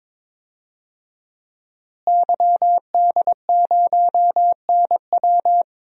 Morse code, heard as a keyed tone, about 22 wpm, 700 Hz. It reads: YD0NW
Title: DXCC-100 Call Signs - 5 Characters - Repeated 3x - 22wpm